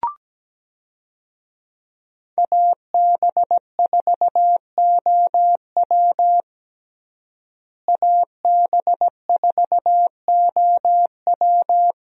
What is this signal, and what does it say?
Morse code, heard as a keyed tone, about 17 wpm, 700 Hz. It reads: AB4OW AB4OW
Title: USA Call Signs - 5 Characters - Repeated 3x - 17wpm